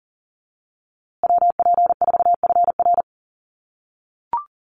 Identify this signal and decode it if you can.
Morse code, heard as a keyed tone, about 40 wpm, 700 Hz. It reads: WP4FR
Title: USA Call Signs Repeated 2x 40wpm